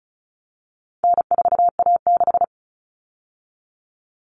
Morse code, heard as a keyed tone, about 34 wpm, 700 Hz. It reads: N4A6
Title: All Cut Numbers - 4-Digit 35wpm